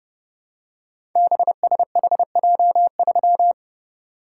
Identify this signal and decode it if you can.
Morse code, heard as a keyed tone, about 30 wpm, 700 Hz. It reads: BSHJ3
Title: Code Groups - 5 Letter:Numbers - 30wpm